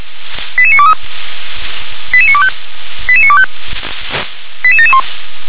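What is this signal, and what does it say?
Five-tone selective call (ZVEI), heard as quick runs of five tones, tones of about 70 ms. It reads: 88923 88925 88925 88771
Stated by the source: ZVEI five-tone selective call 88952 received on SDR.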